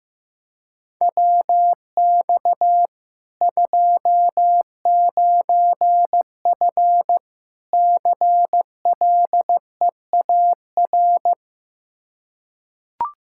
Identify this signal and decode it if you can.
Morse code, heard as a keyed tone, about 16 wpm, 700 Hz. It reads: WX 29F CLEAR
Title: Practice QSO - Exchange 2 - Info - 15wpm